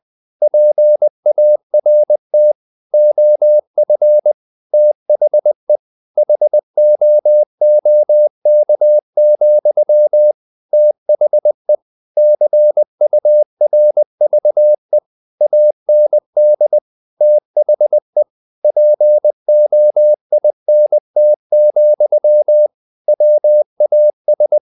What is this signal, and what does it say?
Morse code, heard as a keyed tone, about 20 wpm, 600 Hz. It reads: PART OF THE HOOK, THE CURVE AND THE POINT, WAS
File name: the old man and the sea / the old man and the sea part 1 20wpm